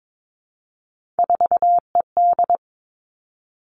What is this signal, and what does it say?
Morse code, heard as a keyed tone, about 22 wpm, 700 Hz. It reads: 4ED
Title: Three Challenging Letter-Number Combinations 22wpm